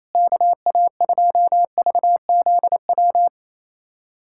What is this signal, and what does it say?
Morse code, heard as a keyed tone, about 28 wpm, 700 Hz. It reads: KA2VZW
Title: USA Callsign DE Callsign 28wpm